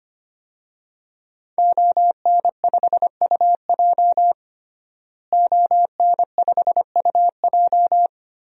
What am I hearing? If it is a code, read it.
Morse code, heard as a keyed tone, about 25 wpm, 700 Hz. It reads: ON5UJ ON5UJ
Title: DXCC-100 Call Signs - 5 Characters - Repeated 2x - 25wpm